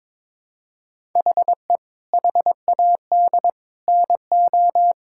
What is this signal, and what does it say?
Morse code, heard as a keyed tone, about 22 wpm, 700 Hz. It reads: HE HAD NO
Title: Sentences from Top 1000 Words - Review Part 6 - 22wpm